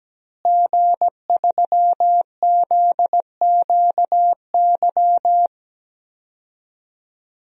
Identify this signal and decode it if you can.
Morse code, heard as a keyed tone, about 17 wpm, 700 Hz. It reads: G3ZQY
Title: DXCC-100 Call Signs - 5 Characters - Repeated 3x - 17wpm